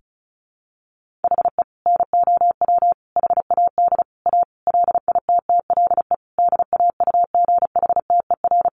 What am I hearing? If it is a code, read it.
Morse code, heard as a keyed tone, about 34 wpm, 700 Hz. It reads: HE NOW HAD A LITTLE DAUGHTER
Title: Sentences from Top 1500 Words 35wpm